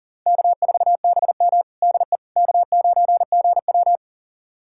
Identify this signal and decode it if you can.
Morse code, heard as a keyed tone, about 40 wpm, 700 Hz. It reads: K4BM DE K9GW
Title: USA Callsign DE Callsign - 4 Characters - 40wpm